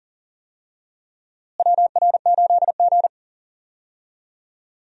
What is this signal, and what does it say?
Morse code, heard as a keyed tone, about 40 wpm, 700 Hz. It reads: WR8G
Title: USA Call Signs - 4 Characters - Repeated 2x - 40wpm